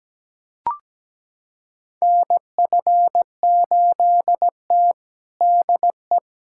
Morse code, heard as a keyed tone, about 17 wpm, 700 Hz. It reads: NF8T DE
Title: USA Callsign DE Callsign - 4 Characters - 17wpm